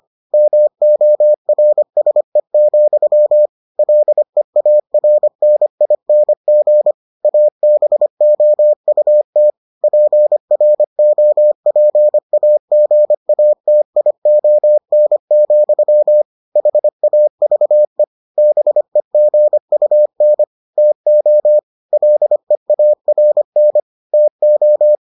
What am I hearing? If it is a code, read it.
Morse code, heard as a keyed tone, about 25 wpm, 600 Hz. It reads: MORSE, LEARNING ABOUT PROPAGATION, HAVE BEGUN TO LEARN TO